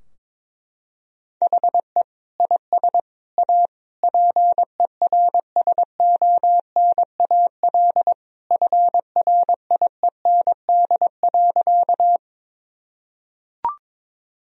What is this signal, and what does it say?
Morse code, heard as a keyed tone, about 22 wpm, 700 Hz. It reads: HE IS A PERSONAL FRIEND.
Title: Sentences from Top 1000 Words - Review Part 4 - 22wpm